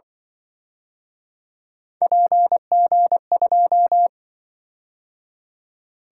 Morse code, heard as a keyed tone, about 24 wpm, 700 Hz. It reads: PG2
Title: Speed-Racing - Three Letter-Number Combinations - 20wpm